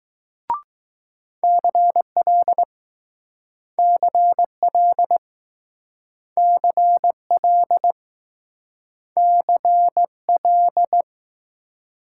Morse code, heard as a keyed tone, about 20 wpm, 700 Hz. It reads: CL CL CL CL